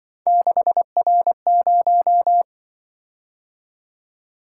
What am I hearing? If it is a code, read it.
Morse code, heard as a keyed tone, about 26 wpm, 700 Hz. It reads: 6R0